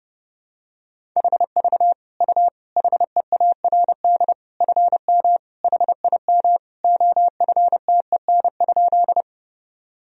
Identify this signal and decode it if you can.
Morse code, heard as a keyed tone, about 30 wpm, 700 Hz. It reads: HV U HEARD FM HIM OFTEN?
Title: Sentences from Top 400 Words and Abbreviations 30wpm